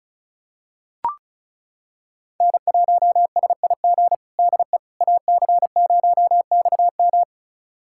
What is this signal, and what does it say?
Morse code, heard as a keyed tone, about 36 wpm, 700 Hz. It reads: N1SIG DE AC0XM
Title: USA Callsign DE Callsign - 5 Characters - 35wpm